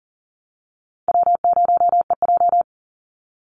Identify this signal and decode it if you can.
Morse code, heard as a keyed tone, about 40 wpm, 700 Hz. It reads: W0EJ
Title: Easy Call Signs Repeated 3x - 40wpm